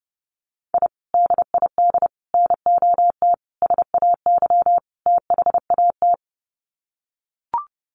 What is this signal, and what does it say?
Morse code, heard as a keyed tone, about 30 wpm, 700 Hz. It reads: I DID NOT SAY THAT
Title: Sets of 5 Words Encore2 30wpm